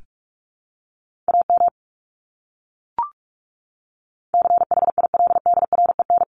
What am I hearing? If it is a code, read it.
Morse code, heard as a keyed tone, about 44 wpm, 700 Hz. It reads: AM CHILDREN
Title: Top 500 Words Repeated 45wpm